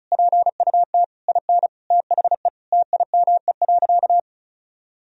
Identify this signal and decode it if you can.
Morse code, heard as a keyed tone, about 34 wpm, 700 Hz. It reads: PUT IN THE TIME.
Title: Sentences from Top 200 Words and Abbreviations 35wpm